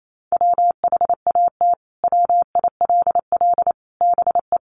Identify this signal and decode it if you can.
Morse code, heard as a keyed tone, about 28 wpm, 700 Hz. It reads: WHAT WILL BE
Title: Sentences from Top 700 Words 28wpm